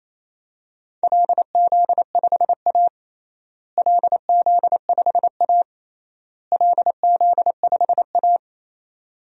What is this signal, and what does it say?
Morse code, heard as a keyed tone, about 28 wpm, 700 Hz. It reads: LZ5A LZ5A LZ5A
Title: DXCC-100 Call Signs - 4 Characters - Repeated 3x - 28wpm